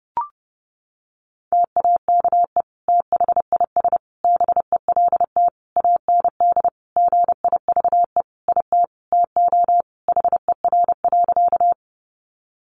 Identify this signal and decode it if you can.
Morse code, heard as a keyed tone, about 30 wpm, 700 Hz. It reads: TAKE THIS BELT AND GIVE IT TO HER.
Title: Sentences from Top 1400 Words 30wpm